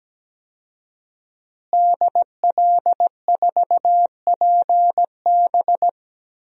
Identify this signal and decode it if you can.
Morse code, heard as a keyed tone, about 17 wpm, 700 Hz. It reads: DL4PB
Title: DXCC-100 Call Signs - 5 Characters - Repeated 2x - 17wpm